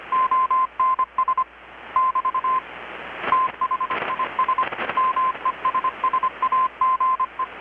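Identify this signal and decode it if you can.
Morse code, heard as a keyed tone, about 25 wpm, 1040 Hz. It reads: ONS <BT> THIS MESSAGE
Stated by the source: Morse from KPH's Night of Nights XX maritime radiotelegraphy event, received in Chiapas, Mexico about 3,800 km away on a Kenwood R-600 with a grounded 10 m long-wire antenna.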